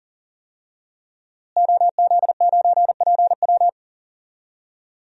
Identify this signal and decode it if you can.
Morse code, heard as a keyed tone, about 40 wpm, 700 Hz. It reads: OZ9PW
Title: DXCC-100 Call Signs - 5 Characters - Repeated 2x - 40wpm